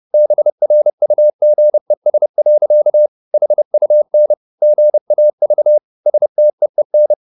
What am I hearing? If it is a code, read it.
Morse code, heard as a keyed tone, about 30 wpm, 600 Hz. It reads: BRUGES. HUN GAV STEEN
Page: Ferie 30 wpm